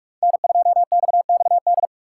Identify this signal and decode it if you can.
Morse code, heard as a keyed tone, about 44 wpm, 700 Hz. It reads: N2XXD